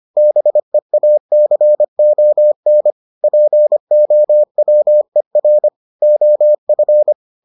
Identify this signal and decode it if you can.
Morse code, heard as a keyed tone, about 25 wpm, 600 Hz. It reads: BEACON POWER OF